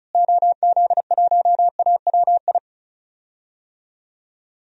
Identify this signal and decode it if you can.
Morse code, heard as a keyed tone, about 36 wpm, 700 Hz. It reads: OZ1AWI
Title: DXCC-100 Call Signs - 6 Characters - Repeated 3x - 35wpm